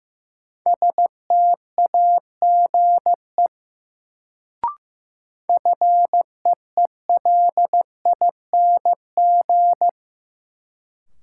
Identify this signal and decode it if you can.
Morse code, heard as a keyed tone, about 15 wpm, 700 Hz. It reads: STAGE FEELING
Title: Sentences from Top 1200 Words - New Word Drill Alt - 15wpm